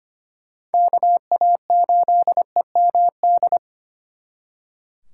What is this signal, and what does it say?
Morse code, heard as a keyed tone, about 32 wpm, 700 Hz. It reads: KA8EMD